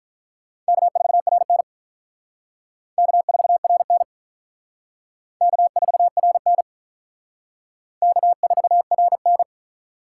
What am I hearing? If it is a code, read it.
Morse code, heard as a keyed tone, about 44 wpm, 700 Hz. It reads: K4RN K4RN K4RN K4RN